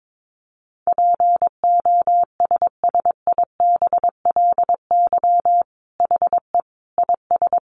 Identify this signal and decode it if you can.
Morse code, heard as a keyed tone, about 22 wpm, 700 Hz. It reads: POSSIBLY HE IS
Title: Sentences from Top 1300 Words 22wpm